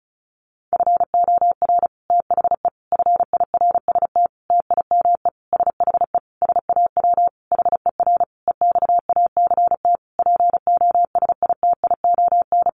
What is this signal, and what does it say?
Morse code, heard as a keyed tone, about 34 wpm, 700 Hz. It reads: FOR THE FIRST TIME SHE SAW HER EXACT POSITION
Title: Sentences from Top 1300 Words 35wpm